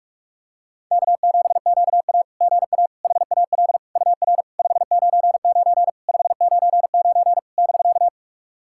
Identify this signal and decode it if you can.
Morse code, heard as a keyed tone, about 44 wpm, 690 Hz. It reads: K7QA GA HAL UR 599 599 <BK>
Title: SOTA QSO Practice 45wpm